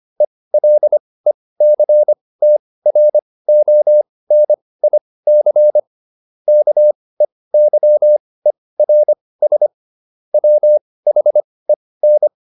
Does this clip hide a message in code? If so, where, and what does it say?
Yes: Morse code from 0.2 to 12.3 s, heard as a keyed tone, about 25 wpm, spaced as for 18 wpm, 600 Hz. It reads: ELECTRONIC KEYERS WHEN